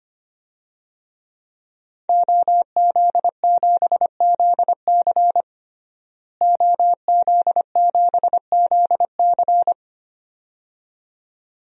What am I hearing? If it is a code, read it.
Morse code, heard as a keyed tone, about 25 wpm, 700 Hz. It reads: OZ7ZC OZ7ZC
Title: DXCC-100 Call Signs - 5 Characters - Repeated 2x - 25wpm